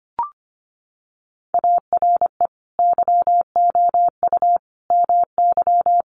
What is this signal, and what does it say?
Morse code, heard as a keyed tone, about 25 wpm, 700 Hz. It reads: ARE YOU MY
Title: Sentences from Top 300 Words 25wpm